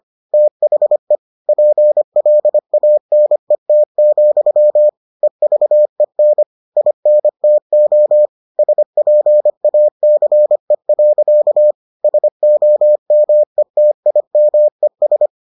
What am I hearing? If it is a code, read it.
Morse code, heard as a keyed tone, about 25 wpm, 600 Hz. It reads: THE PLANET, EVEN INTO SPACE. SOMETIMES